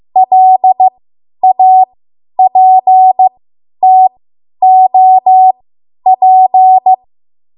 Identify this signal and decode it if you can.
Morse code, heard as a keyed tone, about 15 wpm, spaced as for 10 wpm, 750 Hz. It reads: LAPTOP
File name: Morse / 10WPM